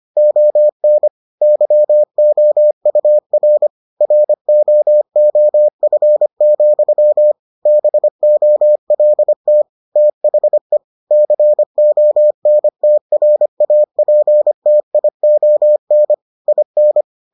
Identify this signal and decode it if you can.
Morse code, heard as a keyed tone, about 25 wpm, 600 Hz. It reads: ON YOUR ROOF, BOLT THE CONTRAPTION IN